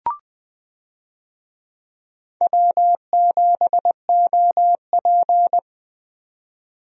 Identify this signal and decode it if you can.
Morse code, heard as a keyed tone, about 20 wpm, 700 Hz. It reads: W7OP
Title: USA Call Signs - 4 Characters - Repeated 3x - 20wpm